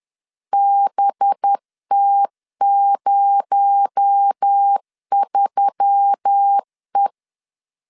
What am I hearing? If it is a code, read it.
Morse code, heard as a keyed tone, about 10 wpm, 800 Hz. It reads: BT03E
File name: LEZIONI / 7 GF9Z / 7 GF9Z 10wpm 2611